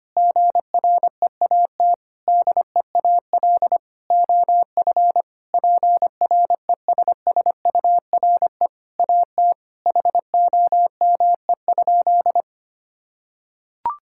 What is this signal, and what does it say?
Morse code, heard as a keyed tone, about 25 wpm, 700 Hz. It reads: GREAT DEAL OF PRESSURE AT HOME?
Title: Sentences from Top 1000 Words - Review Part 6 - 25wpm